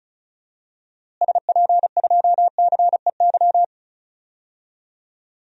Morse code, heard as a keyed tone, about 34 wpm, 700 Hz. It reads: SP2CEY
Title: DXCC-100 Call Signs - 6 Characters - Repeated 3x - 35wpm